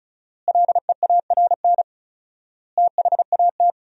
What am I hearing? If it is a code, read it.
Morse code, heard as a keyed tone, about 34 wpm, 700 Hz. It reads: LEARN THAT
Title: 4x Word-Spacing - Sentences from Top 300 Words - 35wpm